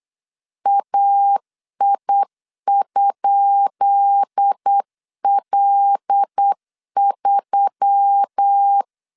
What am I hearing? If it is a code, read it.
Morse code, heard as a keyed tone, about 8 wpm, 800 Hz. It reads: AI?L3
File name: LEZIONI / 10 SIGNS / 10B INT 8wpm 3534